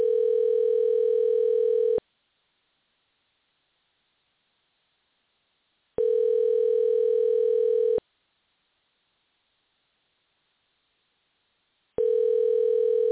Ringback tone (North America), 440 + 480 Hz.
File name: moh / ringback-tone